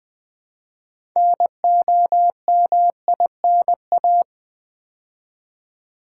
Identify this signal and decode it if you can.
Morse code, heard as a keyed tone, about 21 wpm, 700 Hz. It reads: NOMINA
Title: Build A Word - 2 Letters At A Time - 8-Letter Words - 20wpm